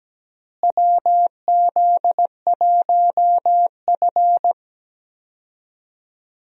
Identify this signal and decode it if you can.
Morse code, heard as a keyed tone, about 17 wpm, 700 Hz. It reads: WZ1F
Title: USA Call Signs - 4 Characters - Repeated 3x - 17wpm